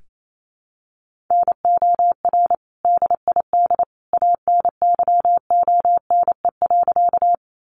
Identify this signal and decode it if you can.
Morse code, heard as a keyed tone, about 28 wpm, 700 Hz. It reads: NOR DID ANYONE.
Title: Sentences from Top 1500 Words 28wpm